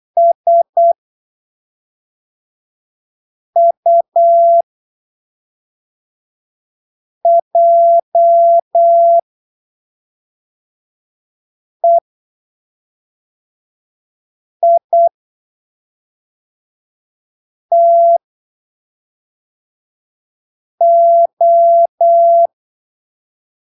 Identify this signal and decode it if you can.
Morse code, heard as a keyed tone, about 8 wpm, spaced as for 3 wpm, 680 Hz. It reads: SUJEITO
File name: arquivos / texto / texto 28 08ppm